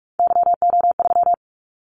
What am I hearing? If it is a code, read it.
Morse code, heard as a keyed tone, about 46 wpm, 700 Hz. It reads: YO3
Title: Callsign Prefixes for Top 25 DXCC - 45wpm